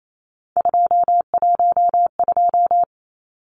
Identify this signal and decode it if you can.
Morse code, heard as a keyed tone, about 28 wpm, 700 Hz. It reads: 212
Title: Numbers - 3-Digit 28wpm